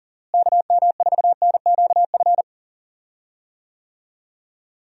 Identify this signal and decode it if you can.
Morse code, heard as a keyed tone, about 40 wpm, 700 Hz. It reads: KM4NQF
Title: USA Call Signs - 6 Characters - Repeated 3x - 40wpm